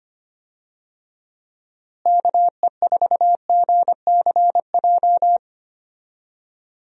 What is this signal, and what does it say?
Morse code, heard as a keyed tone, about 25 wpm, 700 Hz. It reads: KE4GCJ
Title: USA Call Signs - 6 Characters - Repeated 3x - 25wpm